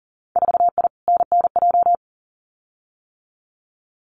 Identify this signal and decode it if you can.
Morse code, heard as a keyed tone, about 40 wpm, 700 Hz. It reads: 4I NNJ
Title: Winter Field Day Exchanges 40wpm